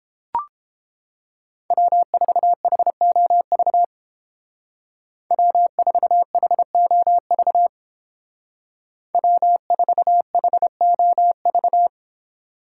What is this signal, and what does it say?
Morse code, heard as a keyed tone, about 30 wpm, 700 Hz. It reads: W4HOV W4HOV W4HOV